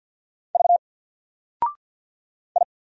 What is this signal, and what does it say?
Morse code, heard as a keyed tone, about 48 wpm, 700 Hz. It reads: V I
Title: Single Character (Letter:Number:Punctuation:Prosign) 50wpm